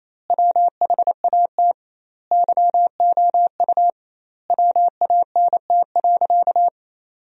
Morse code, heard as a keyed tone, about 28 wpm, 700 Hz. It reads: WHAT YOU WANT.
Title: 2x Word-Spacing - Sentences from Top 200 Words - 28wpm